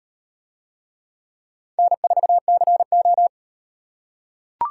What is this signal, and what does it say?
Morse code, heard as a keyed tone, about 38 wpm, 700 Hz. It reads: N4CO